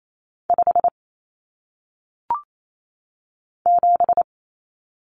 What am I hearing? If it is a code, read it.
Morse code, heard as a keyed tone, about 28 wpm, 700 Hz. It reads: 5 7
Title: Common Cut Numbers - 1-Digit 28wpm